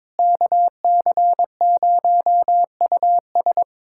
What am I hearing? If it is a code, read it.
Morse code, heard as a keyed tone, about 22 wpm, 700 Hz. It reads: KC0US
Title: USA Callsign DE Callsign - 5 Characters - 22wpm